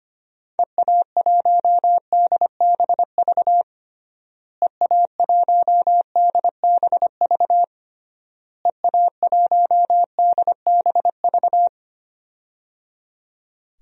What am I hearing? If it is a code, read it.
Morse code, heard as a keyed tone, about 25 wpm, 700 Hz. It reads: EA1DBV EA1DBV EA1DBV
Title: DXCC-100 Call Signs - 6 Characters - Repeated 3x - 25wpm